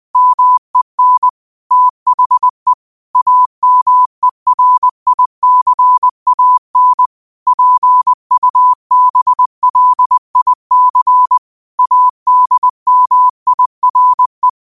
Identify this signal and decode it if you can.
Morse code, heard as a keyed tone, about 20 wpm, 1000 Hz. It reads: MEN THE AMERICAN PUBLIC ADMIRE
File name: 20wpm